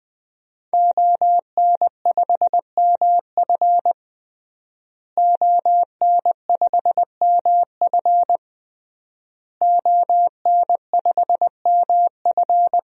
Morse code, heard as a keyed tone, about 20 wpm, 700 Hz. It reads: ON5MF ON5MF ON5MF
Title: DXCC-100 Call Signs - 5 Characters - Repeated 3x - 20wpm